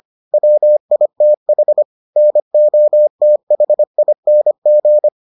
Morse code, heard as a keyed tone, about 25 wpm, 600 Hz. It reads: WITH NOTHING